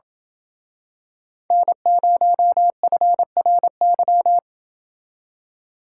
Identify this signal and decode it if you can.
Morse code, heard as a keyed tone, about 27 wpm, 700 Hz. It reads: N0FRY